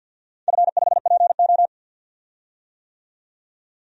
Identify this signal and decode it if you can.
Morse code, heard as a keyed tone, about 52 wpm, 700 Hz. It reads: F5PO
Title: DXCC-100 Call Signs - 4 Characters - 50wpm